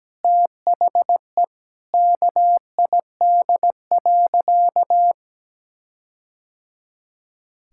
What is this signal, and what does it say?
Morse code, heard as a keyed tone, about 17 wpm, 700 Hz. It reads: THE KID.
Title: Sentences from Top 1600 Words 17wpm